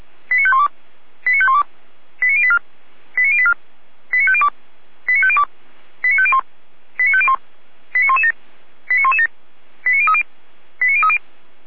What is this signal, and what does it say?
Five-tone selective call (ZVEI), heard as quick runs of five tones, tones of about 70 ms. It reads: 89512 89512 89084 89084 89502 89502 89501 89501 89117 89117 89030 89030
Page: SDR/ZVEI